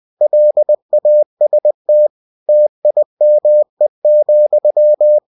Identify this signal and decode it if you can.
Morse code, heard as a keyed tone, about 20 wpm, 600 Hz. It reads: LAST TIME,
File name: cw-books / winnie-the-pooh / winnie-20wpm-page-0031